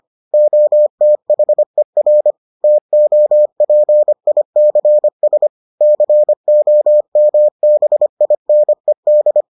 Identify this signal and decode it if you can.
Morse code, heard as a keyed tone, about 25 wpm, 600 Hz. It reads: OTHER TOPICS COMBINED